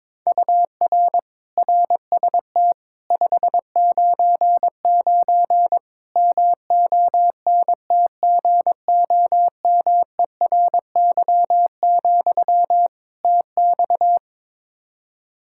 Morse code, heard as a keyed tone, about 22 wpm, 700 Hz. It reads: UR RST 599 MONTGOMERY, TX
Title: K1USN SST QSO Practice 22wpm